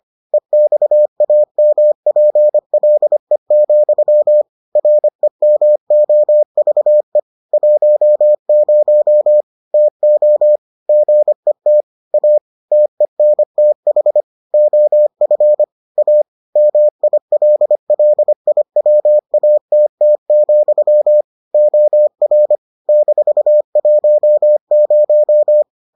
Morse code, heard as a keyed tone, about 25 wpm, 600 Hz. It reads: EXAMPLE, REMOVE 10 TO GET A TENTH OF A MILLIWATT, OR -10